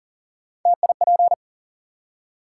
Morse code, heard as a keyed tone, about 40 wpm, 700 Hz. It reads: TIP
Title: Three Letter Words - 40wpm